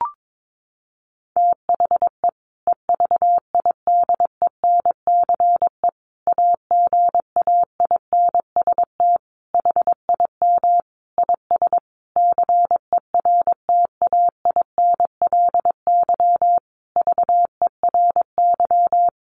Morse code, heard as a keyed tone, about 22 wpm, 700 Hz. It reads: THE EVIDENCE AGAINST HIM IS CERTAINLY VERY
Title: Sentences from Top 1300 Words 22wpm